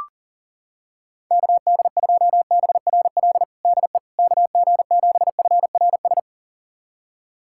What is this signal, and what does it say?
Morse code, heard as a keyed tone, about 40 wpm, 700 Hz. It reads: KD2BRL DE KG7FRS
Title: USA Callsign DE Callsign 40wpm